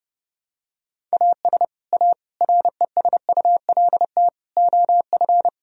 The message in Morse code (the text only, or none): AS A RESULT OF